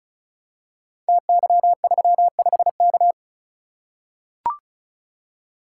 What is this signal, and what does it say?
Morse code, heard as a keyed tone, about 36 wpm, 700 Hz. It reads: TY35K
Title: Code Groups - 5 Letter:Numbers - 35wpm